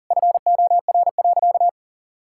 Morse code, heard as a keyed tone, about 37 wpm, 700 Hz. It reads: FOR.
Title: Sentences from Top 1600 Words 40wpm